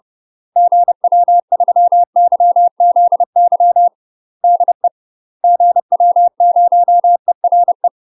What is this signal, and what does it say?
Morse code, heard as a keyed tone, about 30 wpm, 700 Hz. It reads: GW3YZY DE GW0ERE
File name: UK MASTER QSOs TRAINING / 30WPM / UK MASTER QSOs 0031